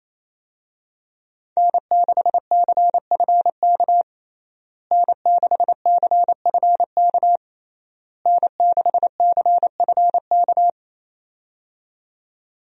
Morse code, heard as a keyed tone, about 28 wpm, 700 Hz. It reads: N6CFK N6CFK N6CFK
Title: USA Call Signs - 5 Characters - Repeated 3x - 28wpm